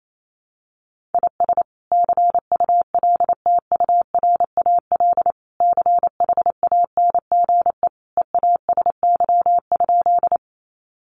Morse code, heard as a keyed tone, about 28 wpm, 700 Hz. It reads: IS CULTURAL CHANGE EASY?
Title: Sentences from Top 1500 Words 28wpm